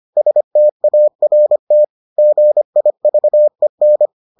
Morse code, heard as a keyed tone, about 25 wpm, 600 Hz. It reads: START GIVEN